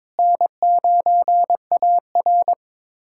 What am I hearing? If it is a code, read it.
Morse code, heard as a keyed tone, about 22 wpm, 700 Hz. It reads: N9AR
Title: USA Callsign DE Callsign - 4 Characters - 22wpm